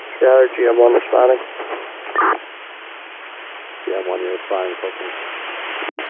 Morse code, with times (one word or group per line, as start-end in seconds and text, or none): none